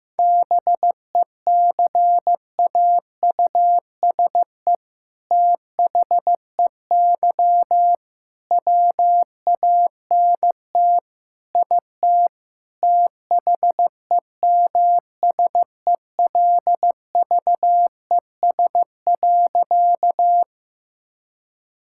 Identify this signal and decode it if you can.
Morse code, heard as a keyed tone, about 16 wpm, 700 Hz. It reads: BECAUSE THEY WANT IT THEMSELVES.
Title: Sentences from Top 1000 Words - Review Part 5 - 15wpm